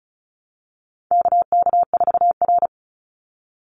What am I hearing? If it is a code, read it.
Morse code, heard as a keyed tone, about 34 wpm, 700 Hz. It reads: KK4R
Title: Call Signs - 35wpm